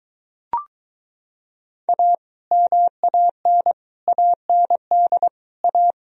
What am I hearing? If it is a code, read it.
Morse code, heard as a keyed tone, about 23 wpm, 700 Hz. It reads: A MAN AND A